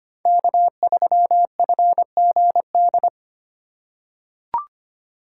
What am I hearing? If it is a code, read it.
Morse code, heard as a keyed tone, about 25 wpm, 700 Hz. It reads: K3FGD
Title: USA Call Signs - 5 Characters - 4x Character Spacing - 25wpm